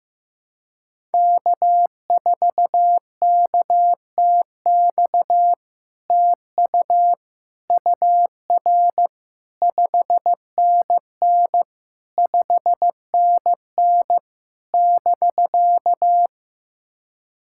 Morse code, heard as a keyed tone, about 15 wpm, 700 Hz. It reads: K4KTX TU UR 5NN 5NN <BK>
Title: SOTA QSO Practice 15wpm